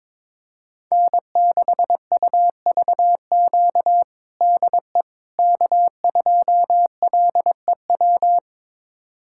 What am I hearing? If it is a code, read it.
Morse code, heard as a keyed tone, about 22 wpm, 700 Hz. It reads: N6UVQ DE K2LEW
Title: USA Callsign DE Callsign - 5 Characters - 22wpm